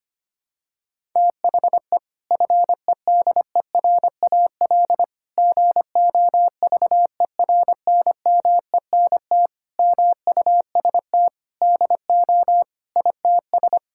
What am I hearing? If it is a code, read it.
Morse code, heard as a keyed tone, about 25 wpm, 700 Hz. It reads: THE FEDERAL GOVERNMENT MUST DO ITS
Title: Sentences from Top 1000 Words - Review Part 5 - 25wpm